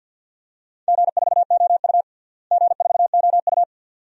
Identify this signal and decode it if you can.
Morse code, heard as a keyed tone, about 51 wpm, 700 Hz. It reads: G4OU G4OU
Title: DXCC-100 Call Signs - 4 Characters - Repeated 2x - 50wpm